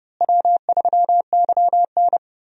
Morse code, heard as a keyed tone, about 32 wpm, 700 Hz. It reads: W3YN
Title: USA Call Signs - 4 Characters - Repeated 3x - 30wpm